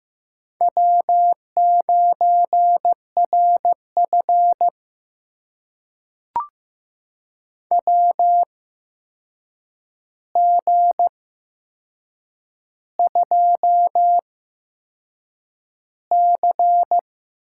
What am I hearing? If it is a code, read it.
Morse code, heard as a keyed tone, about 15 wpm, spaced as for 6 wpm, 700 Hz. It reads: W9RF WG2C
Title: USA Call Signs - 4 Characters - 8x Character Spacing - 15wpm